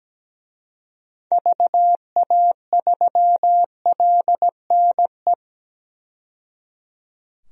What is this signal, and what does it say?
Morse code, heard as a keyed tone, about 17 wpm, 700 Hz. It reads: VA3LNE
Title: DXCC-100 Call Signs - 6 Characters - 17wpm